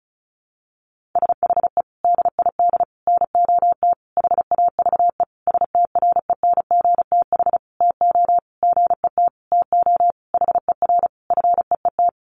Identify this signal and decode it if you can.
Morse code, heard as a keyed tone, about 34 wpm, 700 Hz. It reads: SHE DID NOT HAVE STRENGTH TO GET TO HER FEET
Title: Sentences from Top 1300 Words 35wpm